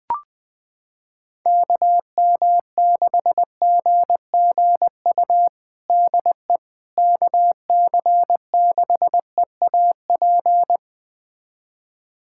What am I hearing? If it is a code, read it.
Morse code, heard as a keyed tone, about 20 wpm, 700 Hz. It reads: KM6GGU DE KC6EAP
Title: USA Callsign DE Callsign 20wpm